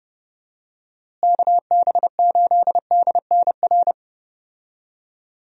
Morse code, heard as a keyed tone, about 31 wpm, 700 Hz. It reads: KB8DNR
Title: USA Call Signs - 6 Characters - Repeated 2x - 30wpm